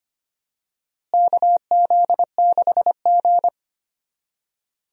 Morse code, heard as a keyed tone, about 25 wpm, 700 Hz. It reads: KZ6G
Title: USA Call Signs - 4 Characters - Repeated 2x - 25wpm